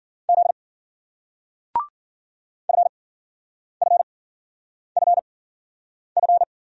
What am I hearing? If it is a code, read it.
Morse code, heard as a keyed tone, about 56 wpm, 700 Hz. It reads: Z F F F F